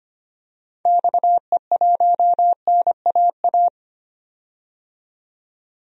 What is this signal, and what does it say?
Morse code, heard as a keyed tone, about 25 wpm, 700 Hz. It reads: XE1NAA
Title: DXCC-100 Call Signs - 6 Characters - 25wpm